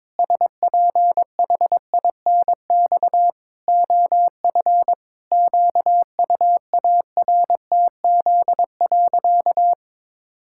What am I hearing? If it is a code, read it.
Morse code, heard as a keyed tone, about 22 wpm, 700 Hz. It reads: SPHINX OF QUARTZ.
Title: Pangram Challenge v2 - 22wpm